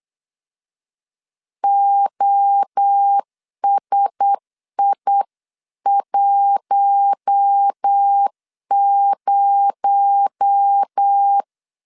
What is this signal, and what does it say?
Morse code, heard as a keyed tone, about 8 wpm, 800 Hz. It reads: OSI10
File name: CW / DET / 1 6 1500C S B 8wpm 5142